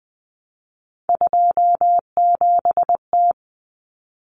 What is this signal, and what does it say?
Morse code, heard as a keyed tone, about 20 wpm, 700 Hz. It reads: 27T
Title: Common Cut Numbers - 3-Digit 20wpm